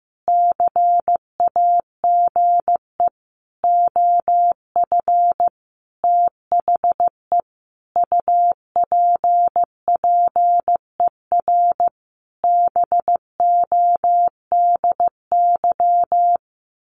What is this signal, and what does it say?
Morse code, heard as a keyed tone, about 14 wpm, 700 Hz. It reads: CAGE OF THE UPPER BODY
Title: A Sound Of Thunder 15wpm